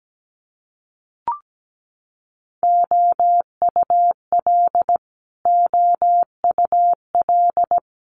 Morse code, heard as a keyed tone, about 17 wpm, 700 Hz. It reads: OUL OUL
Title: Multi-letter Phonemes Repeated 3x 17wpm